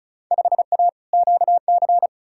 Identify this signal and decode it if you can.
Morse code, heard as a keyed tone, about 36 wpm, 700 Hz. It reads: 5A QC
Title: Field Day Exchanges Encore - 35wpm